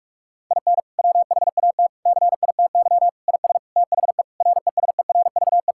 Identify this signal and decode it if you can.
Morse code, heard as a keyed tone, about 44 wpm, 700 Hz. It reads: IN WHAT CITY IS THE RESERVE
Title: Sentences from Top 1000 Words - Review Part 6 - 45wpm